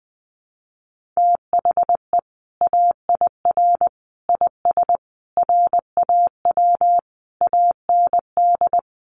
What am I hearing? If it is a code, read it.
Morse code, heard as a keyed tone, about 20 wpm, 700 Hz. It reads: THE AIR IS RAW AND
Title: Sentences from Top 1500 Words 20wpm